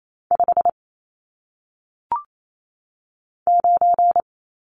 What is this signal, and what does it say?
Morse code, heard as a keyed tone, about 28 wpm, 700 Hz. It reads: 5 9
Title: Numbers - 1-Digit 28wpm